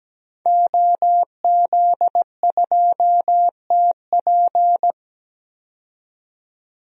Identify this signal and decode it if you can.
Morse code, heard as a keyed tone, about 17 wpm, 700 Hz. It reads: OZ2TP
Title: DXCC-100 Call Signs - 5 Characters - 17wpm